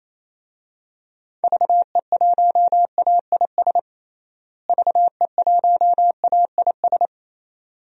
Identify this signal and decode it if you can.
Morse code, heard as a keyed tone, about 28 wpm, 700 Hz. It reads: VE1AIS VE1AIS
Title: DXCC-100 Call Signs - 6 Characters - Repeated 2x - 28wpm